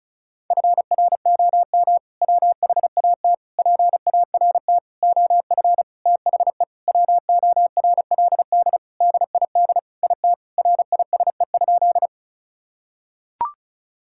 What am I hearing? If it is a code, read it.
Morse code, heard as a keyed tone, about 36 wpm, 700 Hz. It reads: FROM WHAT PART OF THE WORLD DID IT RISE?
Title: Sentences from Top 1000 Words - Review Part 6 - 35wpm